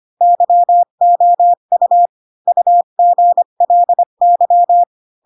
Morse code, heard as a keyed tone, about 25 wpm, 700 Hz. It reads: YOU UGLY
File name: Pinocchio / Pinocchio ch4 20 25 ik1hge